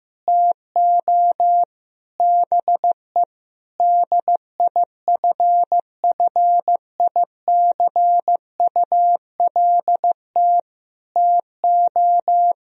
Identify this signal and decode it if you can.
Morse code, heard as a keyed tone, about 15 wpm, 700 Hz. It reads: TO BE DIFFICULT TO
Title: Sentences from Top 1000 Words - Review Part 4 - 15wpm